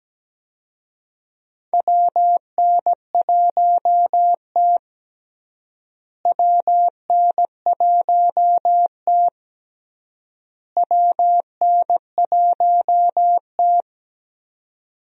Morse code, heard as a keyed tone, about 17 wpm, 700 Hz. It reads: WN1T WN1T WN1T
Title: USA Call Signs - 4 Characters - Repeated 3x - 17wpm